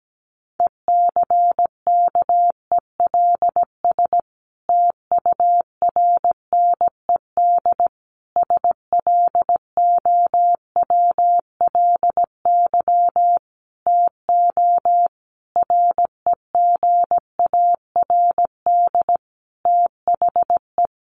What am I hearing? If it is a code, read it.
Morse code, heard as a keyed tone, about 17 wpm, 700 Hz. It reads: ECKELS TURNED SLOWLY TO REGARD THE
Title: A Sound Of Thunder 17wpm